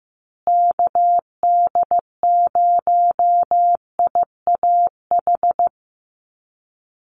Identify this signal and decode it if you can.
Morse code, heard as a keyed tone, about 15 wpm, 700 Hz. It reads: KD0IAH
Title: USA Call Signs Repeated 2x 15wpm